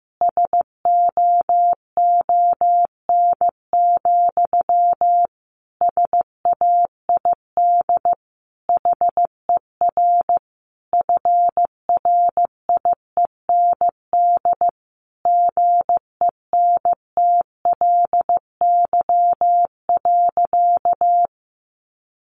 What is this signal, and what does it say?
Morse code, heard as a keyed tone, about 15 wpm, 700 Hz. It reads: SOON, SAID HER FRIEND GENTLY.
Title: Sentences from Top 1500 Words 15wpm